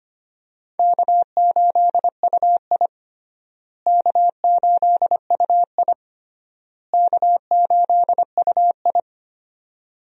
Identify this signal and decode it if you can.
Morse code, heard as a keyed tone, about 25 wpm, 700 Hz. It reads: K8UI K8UI K8UI
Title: USA Call Signs - 4 Characters - Repeated 3x - 25wpm